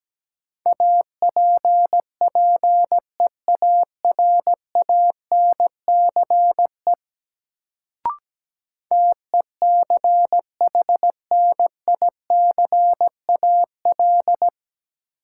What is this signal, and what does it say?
Morse code, heard as a keyed tone, about 17 wpm, 700 Hz. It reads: APPEARANCE TECHNICAL
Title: Sentences from Top 1100 Words - New Word Drill - 17wpm